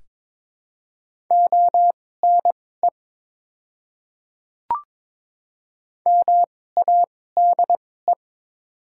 Morse code, heard as a keyed tone, about 22 wpm, spaced as for 16 wpm, 700 Hz. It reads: ONE MADE
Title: 2x Character Spacing - Top 200 Words - 22wpm